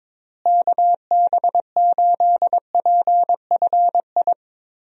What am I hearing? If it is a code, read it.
Morse code, heard as a keyed tone, about 22 wpm, 700 Hz. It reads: KB8PFI
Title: USA Call Signs - 6 Characters - Repeated 2x - 22wpm